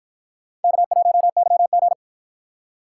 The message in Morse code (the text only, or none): K1YG